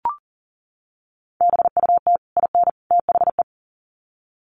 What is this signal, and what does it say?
Morse code, heard as a keyed tone, about 40 wpm, 700 Hz. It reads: BUT IN THE
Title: Sets of 3 Words 40wpm